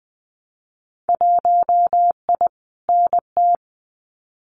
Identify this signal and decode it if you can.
Morse code, heard as a keyed tone, about 20 wpm, 700 Hz. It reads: 1I NT
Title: Winter Field Day Exchanges Encore 20wpm